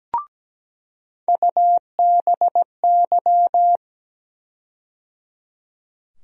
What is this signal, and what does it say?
Morse code, heard as a keyed tone, about 17 wpm, 700 Hz. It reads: UBY
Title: USA Call Sign Postfixes - 3 Letters 17wpm